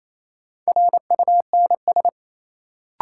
Morse code, heard as a keyed tone, about 28 wpm, 700 Hz. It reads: RUNS
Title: Sentences from Top 600 Words - New Word Drill - 28wpm